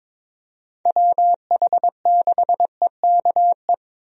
Morse code, heard as a keyed tone, about 22 wpm, 700 Hz. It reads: WH6EKE
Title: USA Call Signs - 6 Characters - Repeated 2x - 22wpm